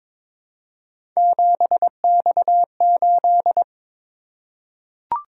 Morse code, heard as a keyed tone, about 22 wpm, 700 Hz. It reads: 7X8
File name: practice / 22wpm / Three Letter-Number Combinations - 22wpm